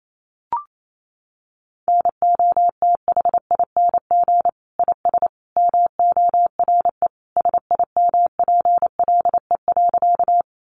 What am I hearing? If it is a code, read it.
Morse code, heard as a keyed tone, about 28 wpm, 700 Hz. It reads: NOTHING IS MORE SIMPLE.
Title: Sentences from Top 500 Words 28wpm